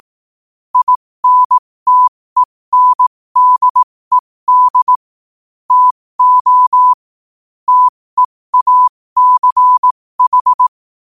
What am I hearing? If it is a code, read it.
Morse code, heard as a keyed tone, about 18 wpm, spaced as for 15 wpm, 1000 Hz. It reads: INTENDED TO TEACH
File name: qotdfiles / 15wpm